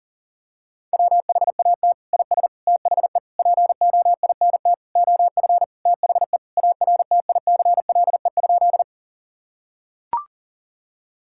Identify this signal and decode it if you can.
Morse code, heard as a keyed tone, about 40 wpm, 700 Hz. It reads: WHAT IS THE POINT OF THE ARTICLE?
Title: Sentences from Top 1000 Words - Review Part 5 - 40wpm